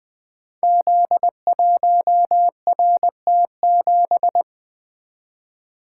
Morse code, heard as a keyed tone, about 20 wpm, 700 Hz. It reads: Z1RT7
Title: Code Groups - 5 Letter:Numbers - 20wpm